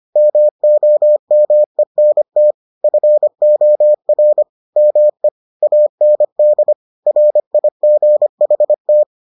Morse code, heard as a keyed tone, about 25 wpm, 600 Hz. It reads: MOMENT FOR ME AND RIGHT